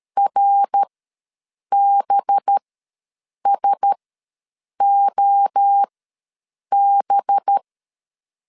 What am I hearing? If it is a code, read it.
Morse code, heard as a keyed tone, about 13 wpm, spaced as for 7 wpm, 800 Hz. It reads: RBSOB
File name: 5 LB ws 12wpm